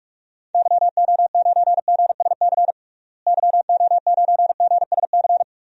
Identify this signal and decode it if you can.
Morse code, heard as a keyed tone, about 46 wpm, 700 Hz. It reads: YO9GSC YO9GSC
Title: DXCC-100 Call Signs - 6 Characters - Repeated 3x - 45wpm